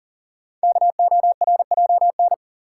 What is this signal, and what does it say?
Morse code, heard as a keyed tone, about 39 wpm, 690 Hz. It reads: KORJN
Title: Code Groups - 5 Letters - 40wpm